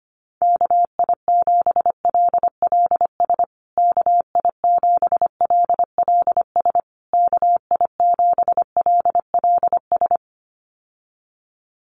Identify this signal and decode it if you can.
Morse code, heard as a keyed tone, about 25 wpm, 700 Hz. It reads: KI7LLS KI7LLS KI7LLS
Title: USA Call Signs Repeated 3x 25wpm